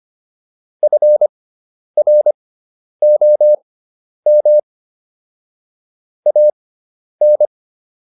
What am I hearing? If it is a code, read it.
Morse code, heard as a keyed tone, about 25 wpm, spaced as for 10 wpm, 600 Hz. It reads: FROM AN